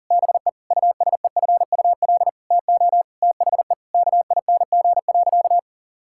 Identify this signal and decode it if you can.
Morse code, heard as a keyed tone, about 37 wpm, 700 Hz. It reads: BE USEFUL TO THE KING.
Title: Sentences from Top 1000 Words - Review Part 6 - 40wpm